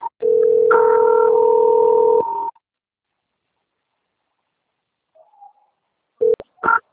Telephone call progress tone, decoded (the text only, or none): ringback tone